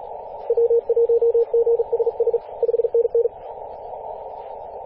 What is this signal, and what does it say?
Morse code, heard as a keyed tone, about 39 wpm, 470 Hz, with radio noise. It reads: W1GSS 5NN
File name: posts / 2016-apr-22-2153z-kh8-kc0w-18-080mhz-cw